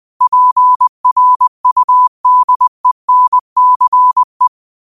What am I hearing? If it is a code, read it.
Morse code, heard as a keyed tone, about 20 wpm, 1000 Hz. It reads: PRUDENCE